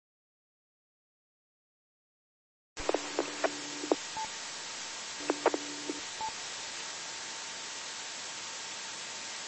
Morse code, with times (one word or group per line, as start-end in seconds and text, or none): none